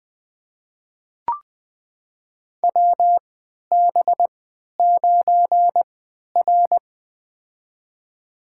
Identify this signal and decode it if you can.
Morse code, heard as a keyed tone, about 20 wpm, spaced as for 11 wpm, 700 Hz. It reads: WB9R
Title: USA Call Signs - 3x Character Spacing - 20wpm